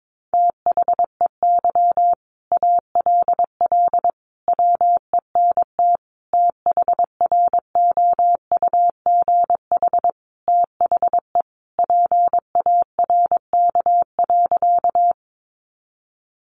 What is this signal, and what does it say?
Morse code, heard as a keyed tone, about 22 wpm, 700 Hz. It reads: THEY ALL WENT THROUGH THE PARK.
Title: Sentences from Top 1300 Words 22wpm